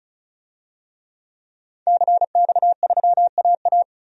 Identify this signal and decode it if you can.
Morse code, heard as a keyed tone, about 36 wpm, 700 Hz. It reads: CX3AA
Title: DXCC-100 Call Signs - 5 Characters - 35wpm